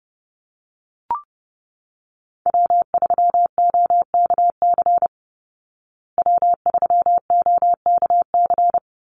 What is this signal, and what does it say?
Morse code, heard as a keyed tone, about 30 wpm, 700 Hz. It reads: W3OKC W3OKC
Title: Call Signs Repeated 2x - 30wpm